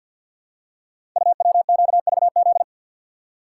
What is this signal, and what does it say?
Morse code, heard as a keyed tone, about 52 wpm, 690 Hz. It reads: UWQVZ
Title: Code Groups - 5 Letters - 50wpm